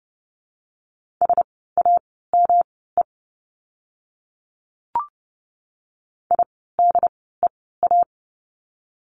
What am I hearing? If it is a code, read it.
Morse code, heard as a keyed tone, about 30 wpm, spaced as for 17 wpm, 700 Hz. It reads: SAME IDEA
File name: course / 30:10wpm / 108 Lesson 18f - Letter m - Words 30wpm